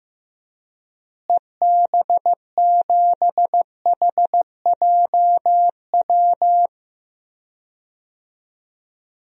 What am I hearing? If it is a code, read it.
Morse code, heard as a keyed tone, about 15 wpm, 700 Hz. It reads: EB7HJW
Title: DXCC-100 Call Signs - 6 Characters - Repeated 2x - 15wpm